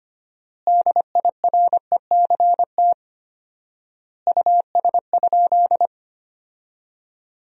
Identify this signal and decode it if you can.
Morse code, heard as a keyed tone, about 25 wpm, 700 Hz. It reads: DIRECT US?
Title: 4x Word-Spacing - Sentences from Top 400 Words - 25wpm